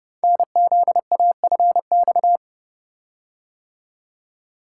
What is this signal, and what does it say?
Morse code, heard as a keyed tone, about 30 wpm, 700 Hz. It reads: NZAFX
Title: Code Groups - 5 Letters - 30wpm